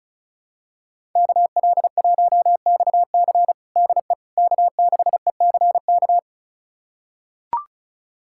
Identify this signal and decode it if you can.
Morse code, heard as a keyed tone, about 36 wpm, 700 Hz. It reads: KL1XC DE K6ECK
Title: USA Callsign DE Callsign - 5 Characters - 35wpm